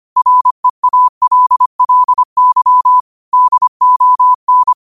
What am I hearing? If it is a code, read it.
Morse code, heard as a keyed tone, about 25 wpm, 1000 Hz. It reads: REALLY DON
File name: qotdfiles / 25wpm